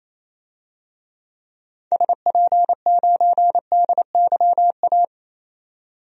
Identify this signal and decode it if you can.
Morse code, heard as a keyed tone, about 28 wpm, 700 Hz. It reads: SP9DYA
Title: DXCC-100 Call Signs - 6 Characters - 28wpm